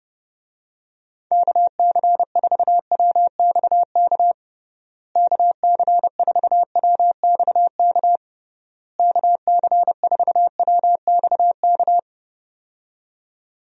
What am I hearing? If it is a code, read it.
Morse code, heard as a keyed tone, about 30 wpm, 700 Hz. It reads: KC4WXK KC4WXK KC4WXK
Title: USA Call Signs - 6 Characters - Repeated 3x - 30wpm